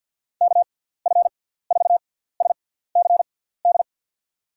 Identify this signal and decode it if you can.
Morse code, heard as a keyed tone, about 48 wpm, spaced as for 17 wpm, 700 Hz. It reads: KF4SCD
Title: USA Call Signs - 6 Characters - 6x Character Spacing - 50wpm